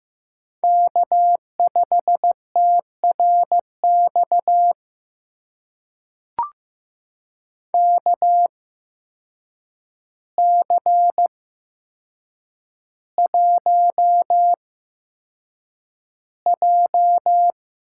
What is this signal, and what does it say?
Morse code, heard as a keyed tone, about 15 wpm, spaced as for 7 wpm, 700 Hz. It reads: K5TRX KC1J
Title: USA Call Signs - 8x Character Spacing - 15wpm